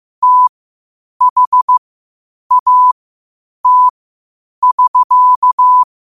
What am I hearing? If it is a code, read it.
Morse code, heard as a keyed tone, about 15 wpm, spaced as for 8 wpm, 1000 Hz. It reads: THAT <SK>
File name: 8wpm